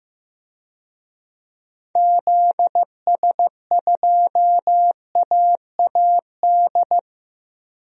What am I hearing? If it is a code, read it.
Morse code, heard as a keyed tone, about 15 wpm, 700 Hz. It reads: ZS2AAD
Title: DXCC-100 Call Signs - 6 Characters - 15wpm